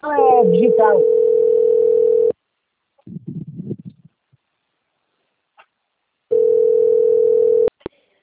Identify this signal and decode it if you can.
Ringback tone (North America), 440 + 480 Hz, with speech.
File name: out- - - - - .10610